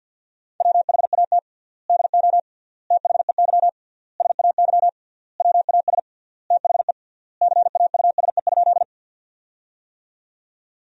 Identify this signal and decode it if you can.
Morse code, heard as a keyed tone, about 50 wpm, 700 Hz. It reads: WHAT DO THEY SAY WAS THE CAUSE?
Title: 3x Word-Spacing - Sentences from Top 200 Words - 50wpm